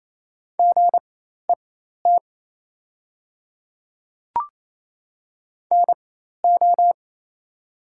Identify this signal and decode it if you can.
Morse code, heard as a keyed tone, about 27 wpm, spaced as for 13 wpm, 690 Hz. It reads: GET NO
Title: 4x Character Spacing - Top 200 Words - 28wpm